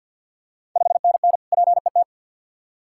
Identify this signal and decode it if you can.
Morse code, heard as a keyed tone, about 49 wpm, 700 Hz. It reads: 5NN PET
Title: Ontario QSO Party Exchange 50wpm